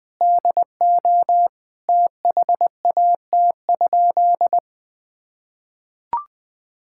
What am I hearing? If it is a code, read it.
Morse code, heard as a keyed tone, about 20 wpm, 700 Hz. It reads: DO THAT?
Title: Sentences from Top 1000 Words - Review Part 6 - 20wpm